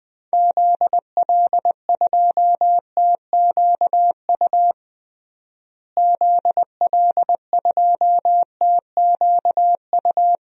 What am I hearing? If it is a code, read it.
Morse code, heard as a keyed tone, about 20 wpm, 700 Hz. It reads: ZL2TQU ZL2TQU
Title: DXCC-100 Call Signs - 6 Characters - Repeated 2x - 20wpm